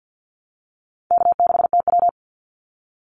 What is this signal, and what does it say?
Morse code, heard as a keyed tone, about 50 wpm, 700 Hz. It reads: K6TW
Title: NA SOTA Activator Callsigns 50wpm